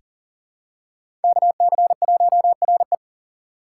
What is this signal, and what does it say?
Morse code, heard as a keyed tone, about 40 wpm, 700 Hz. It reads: KC1RE
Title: USA Call Signs - 3x Character Spacing - 40wpm